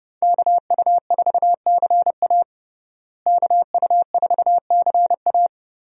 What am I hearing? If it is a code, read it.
Morse code, heard as a keyed tone, about 29 wpm, 700 Hz. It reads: KU4CA KU4CA
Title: USA Call Signs - 5 Characters - Repeated 3x - 30wpm